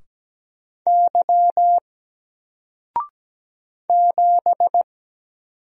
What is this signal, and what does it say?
Morse code, heard as a keyed tone, about 17 wpm, 700 Hz. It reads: Y 7
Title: Single Letter-Number 17wpm